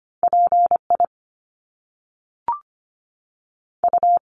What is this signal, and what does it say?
Morse code, heard as a keyed tone, about 25 wpm, 700 Hz. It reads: P I U
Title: Top Two Letter Combinations in Words 25wpm